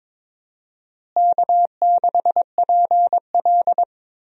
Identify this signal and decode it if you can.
Morse code, heard as a keyed tone, about 22 wpm, 700 Hz. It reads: K6PL
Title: USA Call Signs - 4 Characters - Repeated 2x - 22wpm